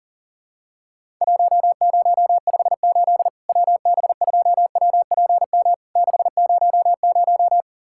Morse code, heard as a keyed tone, about 40 wpm, 700 Hz. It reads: 1058 WB2WPM 600
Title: Worked All Europe DX Contest - QTC Exchange - 40wpm